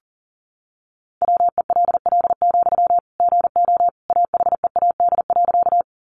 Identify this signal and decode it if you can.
Morse code, heard as a keyed tone, about 40 wpm, 700 Hz. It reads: WELL, GO AHEAD.
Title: Sentences from Top 1100 Words 40wpm